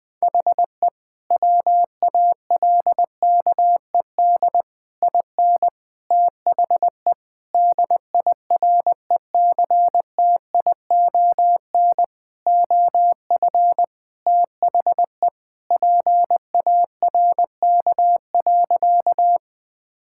Morse code, heard as a keyed tone, about 20 wpm, 700 Hz. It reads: HE WALKED IN THE DIRECTION OF THE PARK.
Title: Sentences with 5-Word Phrases - 20wpm